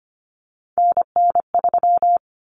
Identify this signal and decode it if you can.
Morse code, heard as a keyed tone, about 25 wpm, 700 Hz. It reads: NN3
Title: Common Cut Numbers - 3-Digit 25wpm